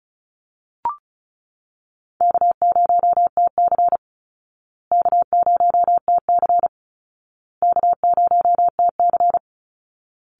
Morse code, heard as a keyed tone, about 34 wpm, 700 Hz. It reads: K0TC K0TC K0TC
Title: Call Signs Repeated 2x - 35wpm